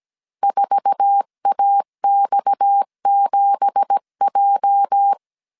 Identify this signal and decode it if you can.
Morse code, heard as a keyed tone, about 17 wpm, 800 Hz. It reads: 4AX7J
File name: 1500C S B 16wpm